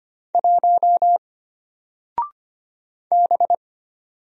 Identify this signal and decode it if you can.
Morse code, heard as a keyed tone, about 25 wpm, 700 Hz. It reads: 1 B
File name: Single-Letter-Number-25wpm